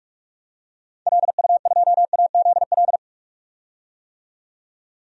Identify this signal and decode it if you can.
Morse code, heard as a keyed tone, about 42 wpm, 700 Hz. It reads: LU2AZL
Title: DXCC-100 Call Signs - 6 Characters - 45wpm